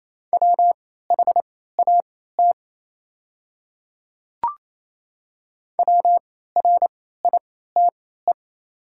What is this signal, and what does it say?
Morse code, heard as a keyed tone, about 28 wpm, spaced as for 16 wpm, 700 Hz. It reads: WHAT WRITE
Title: 3x Character Spacing - Top 100 Words - 28wpm